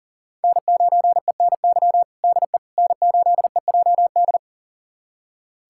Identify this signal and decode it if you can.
Morse code, heard as a keyed tone, about 40 wpm, 700 Hz. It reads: N9ENY DE N8EJD
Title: USA Callsign DE Callsign - 5 Characters - 40wpm